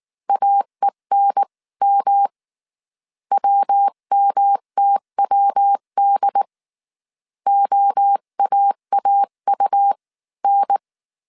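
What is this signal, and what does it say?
Morse code, heard as a keyed tone, about 19 wpm, 800 Hz. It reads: AENM WMTWD OAAU N
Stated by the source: Morse code dictation at 18 wpm from the IW3SGT QRS Morse code training course, five-character groups of 300 characters.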